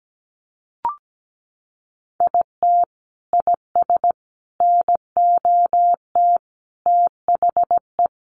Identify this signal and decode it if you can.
Morse code, heard as a keyed tone, about 17 wpm, 700 Hz. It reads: IT IS NOT THE
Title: Sentences from Top 1100 Words 17wpm